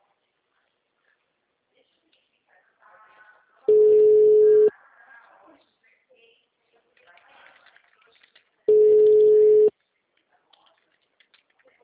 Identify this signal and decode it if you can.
Ringback tone (Europe, 425 Hz), with speech.